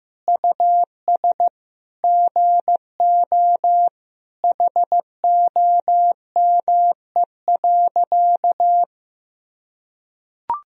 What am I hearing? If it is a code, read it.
Morse code, heard as a keyed tone, about 15 wpm, 700 Hz. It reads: US GO HOME.
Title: Sentences from Top 300 Words and Abbreviations 15wpm